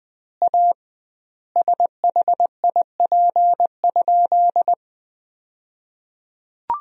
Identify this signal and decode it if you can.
Morse code, heard as a keyed tone, about 20 wpm, 700 Hz. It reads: A SHIP?
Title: 2x Word-Spacing - Sentences from Top 400 Words - 20wpm